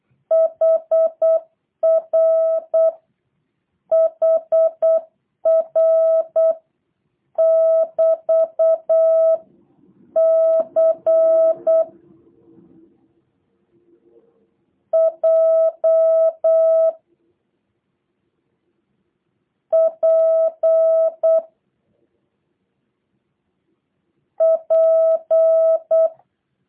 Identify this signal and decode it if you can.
Morse code, heard as a keyed tone, about 8 wpm, 650 Hz. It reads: HR HR <BT> C J P P
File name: jh8cbh / 2023cw / review1-5